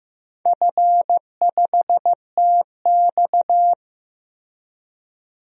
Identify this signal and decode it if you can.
Morse code, heard as a keyed tone, about 16 wpm, 700 Hz. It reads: F5TX
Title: DXCC-100 Call Signs - 4 Characters - Repeated 2x - 15wpm